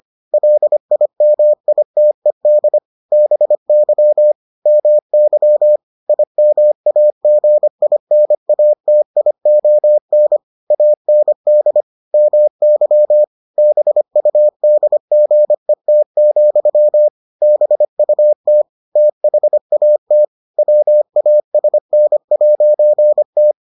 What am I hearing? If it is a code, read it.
Morse code, heard as a keyed tone, about 25 wpm, 600 Hz. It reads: LIMITED BY MY IMAGINATION AND MY BUDGET, BUT THAT WASN'T